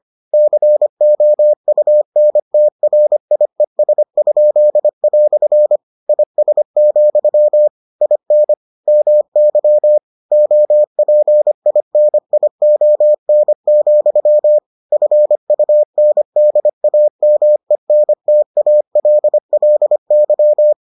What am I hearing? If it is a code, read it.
Morse code, heard as a keyed tone, about 25 wpm, 600 Hz. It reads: COUNTRIES?" IS, IN MY OPINION, FUNDAMENTALLY